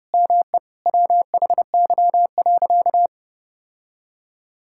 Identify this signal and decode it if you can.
Morse code, heard as a keyed tone, about 32 wpm, 700 Hz. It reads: ME WHY.
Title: Sentences from Top 1000 Words - Review Part 6 - 30wpm